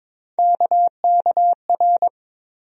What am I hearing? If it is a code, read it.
Morse code, heard as a keyed tone, about 22 wpm, 700 Hz. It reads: KKR
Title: USA Call Sign Postfixes - 3 Letters 22wpm